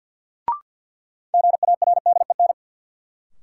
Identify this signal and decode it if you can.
Morse code, heard as a keyed tone, about 52 wpm, 700 Hz. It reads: GARDEN
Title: Sentences from Top 700 Words - New Word Drill - 50wpm